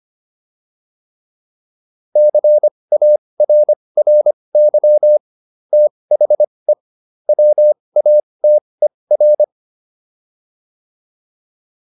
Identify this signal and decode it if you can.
Morse code, heard as a keyed tone, about 25 wpm, spaced as for 20 wpm, 600 Hz. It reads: CARRY THE WATER